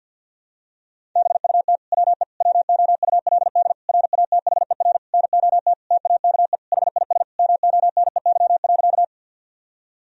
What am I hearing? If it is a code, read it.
Morse code, heard as a keyed tone, about 50 wpm, 700 Hz. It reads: BUT WE WOULD RATHER NOT TAKE HIS MONEY.
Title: Sentences from Top 1000 Words - Review Part 4 - 50wpm